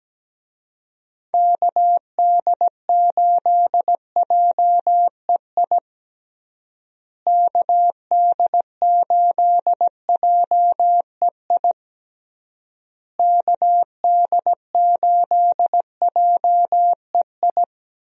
Morse code, heard as a keyed tone, about 17 wpm, 700 Hz. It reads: KD8JEI KD8JEI KD8JEI
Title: USA Call Signs - 6 Characters - Repeated 3x - 17wpm